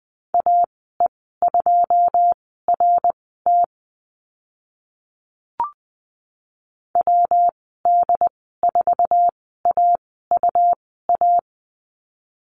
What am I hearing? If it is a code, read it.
Morse code, heard as a keyed tone, about 20 wpm, spaced as for 15 wpm, 700 Hz. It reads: AE2RT WD4AUA
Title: 154 Lesson 25g - Letter p - Call Signs 20wpm